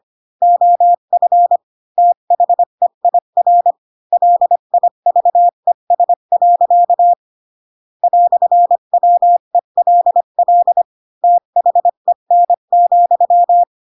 Morse code, heard as a keyed tone, about 25 wpm, 700 Hz. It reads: OF THEIR LIVES. "WELL THEN,